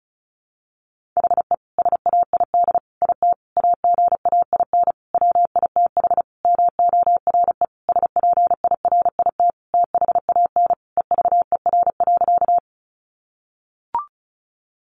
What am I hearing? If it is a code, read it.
Morse code, heard as a keyed tone, about 34 wpm, 700 Hz. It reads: HE SAID IT AGAIN WITH MORE SPIRIT THAN EVER.
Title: Sentences from Top 1400 Words 35wpm